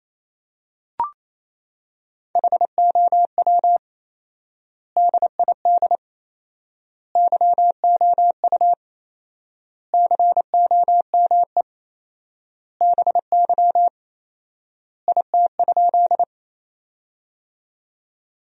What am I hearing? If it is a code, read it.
Morse code, heard as a keyed tone, about 27 wpm, 700 Hz. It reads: HOW DID YOU COME BY IT?
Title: 4x Word-Spacing - Sentences from Top 100 Words - 28wpm